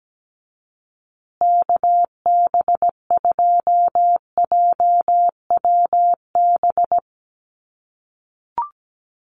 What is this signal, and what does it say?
Morse code, heard as a keyed tone, about 17 wpm, 700 Hz. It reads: KB2JWB
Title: USA Call Signs Repeated 3x 17wpm